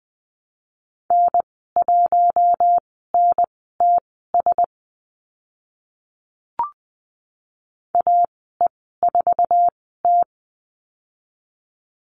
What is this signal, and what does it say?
Morse code, heard as a keyed tone, about 20 wpm, spaced as for 14 wpm, 700 Hz. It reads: N1NTS AE4T
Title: 062 Lesson 11g - Letter h - Call Signs 20wpm